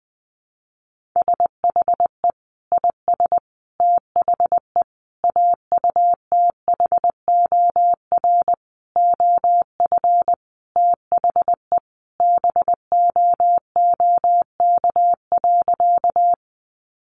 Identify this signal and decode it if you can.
Morse code, heard as a keyed tone, about 20 wpm, 700 Hz. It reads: SHE IS THE AUTHOR OF THE BOOK.
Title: Sentences from Top 1200 Words 20wpm